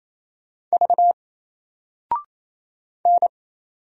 Morse code, heard as a keyed tone, about 27 wpm, 700 Hz. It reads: V N
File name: Single-Letter-Number-28wpm